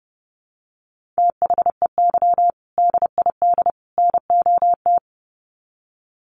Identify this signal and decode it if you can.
Morse code, heard as a keyed tone, about 30 wpm, 700 Hz. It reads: THEY DID NOT
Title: Sets of 3 Words 30wpm